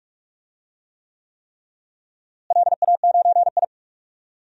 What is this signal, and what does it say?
Morse code, heard as a keyed tone, about 42 wpm, 700 Hz. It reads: LA9I
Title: DXCC-100 Call Signs - 4 Characters - Repeated 3x - 45wpm